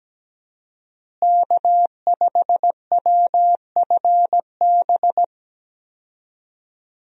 Morse code, heard as a keyed tone, about 17 wpm, 700 Hz. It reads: K5WFB
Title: USA Call Signs - 5 Characters - Repeated 2x - 17wpm